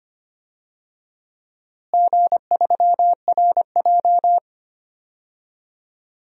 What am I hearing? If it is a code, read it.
Morse code, heard as a keyed tone, about 25 wpm, 700 Hz. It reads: G3RJ
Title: DXCC-100 Call Signs - 4 Characters - Repeated 3x - 25wpm